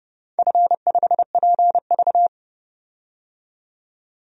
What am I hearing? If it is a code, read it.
Morse code, heard as a keyed tone, about 30 wpm, 700 Hz. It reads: F5PV
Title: DXCC-100 Call Signs - 4 Characters - 30wpm